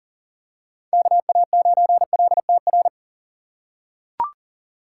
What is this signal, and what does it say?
Morse code, heard as a keyed tone, about 40 wpm, 700 Hz. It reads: KA9LTR
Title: USA Call Signs - 2x Character Spacing - 40wpm